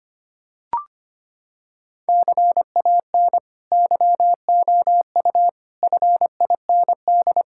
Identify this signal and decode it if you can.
Morse code, heard as a keyed tone, about 25 wpm, 700 Hz. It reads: CAN YOU FIND
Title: Sentences from Top 1600 Words 25wpm